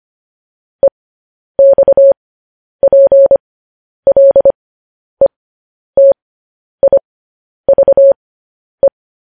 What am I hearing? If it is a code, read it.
Morse code, heard as a keyed tone, about 25 wpm, spaced as for 10 wpm, 550 Hz. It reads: EXPLETIVE